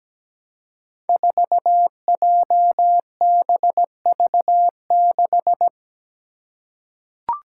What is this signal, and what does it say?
Morse code, heard as a keyed tone, about 17 wpm, 700 Hz. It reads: 4JBV6
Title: Five Challenging Letter-Number Combinations 17wpm